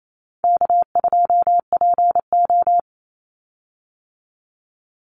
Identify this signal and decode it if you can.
Morse code, heard as a keyed tone, about 28 wpm, 700 Hz. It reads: K2PO
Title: Call Signs Repeated 3x - 28wpm